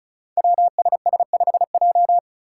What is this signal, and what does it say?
Morse code, heard as a keyed tone, about 36 wpm, 700 Hz. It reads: WSS5J
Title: Five Challenging Letter-Number Combinations 35wpm